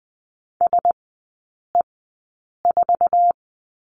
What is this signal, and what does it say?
Morse code, heard as a keyed tone, about 20 wpm, spaced as for 8 wpm, 700 Hz. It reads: SE4
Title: 047 Lesson 9e - Number 4 - Three Characters 20wpm